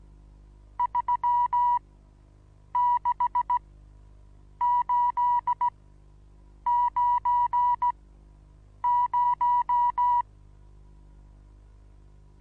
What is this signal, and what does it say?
Morse code, heard as a keyed tone, about 17 wpm, spaced as for 7 wpm, 980 Hz. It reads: 36890
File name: Onlinelehrgang / morsen / DJ4UF Morsen25